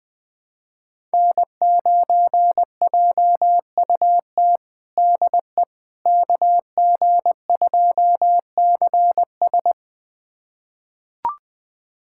Morse code, heard as a keyed tone, about 20 wpm, 700 Hz. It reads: N9JUT DE KG2CS
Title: USA Callsign DE Callsign - 5 Characters - 20wpm